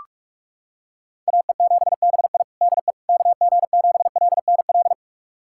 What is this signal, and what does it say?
Morse code, heard as a keyed tone, about 41 wpm, 700 Hz. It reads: AE7BI DE KG7LNL
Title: USA Callsign DE Callsign 45wpm